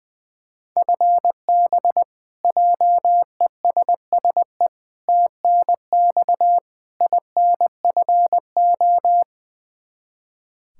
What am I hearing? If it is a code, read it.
Morse code, heard as a keyed tone, about 20 wpm, 700 Hz. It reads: FB JESSE TNX INFO
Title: Practice QSO - Exchange 3 - More Info - 20wpm